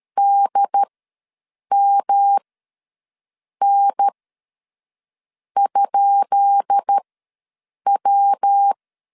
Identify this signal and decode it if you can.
Morse code, heard as a keyed tone, about 13 wpm, spaced as for 6 wpm, 800 Hz. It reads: DMN?W
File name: LEZIONI / 10 SIGNS / 10B INT ws 12wpm 3527